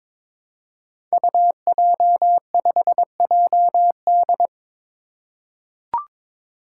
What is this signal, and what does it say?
Morse code, heard as a keyed tone, about 22 wpm, 700 Hz. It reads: UJ5JD
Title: Code Groups - 5 Letter:Numbers - 22wpm